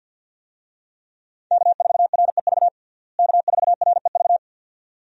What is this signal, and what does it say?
Morse code, heard as a keyed tone, about 49 wpm, 700 Hz. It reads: K4REV K4REV
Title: USA Call Signs - 5 Characters - Repeated 2x - 50wpm